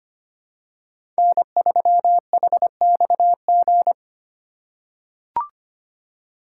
Morse code, heard as a keyed tone, about 25 wpm, 700 Hz. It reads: N3HXG
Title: USA Call Signs - 5 Characters - 3x Character Spacing - 25wpm